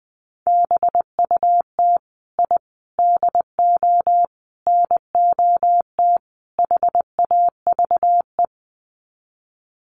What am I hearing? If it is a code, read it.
Morse code, heard as a keyed tone, about 20 wpm, 700 Hz. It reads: BUT I DO NOT HAVE
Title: Sets of 5 Words Encore 20wpm